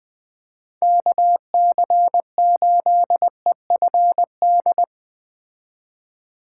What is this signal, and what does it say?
Morse code, heard as a keyed tone, about 19 wpm, 700 Hz. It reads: KC8EFD
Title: Speed Racing - USA Call Signs 17wpm